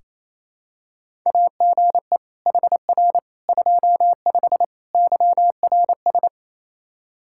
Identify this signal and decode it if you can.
Morse code, heard as a keyed tone, about 28 wpm, 700 Hz. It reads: AGE HR 25 YRS
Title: K1USN SST QSO Practice 28wpm